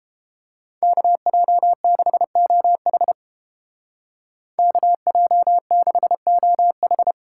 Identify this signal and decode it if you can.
Morse code, heard as a keyed tone, about 31 wpm, 700 Hz. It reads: KJ6OH KJ6OH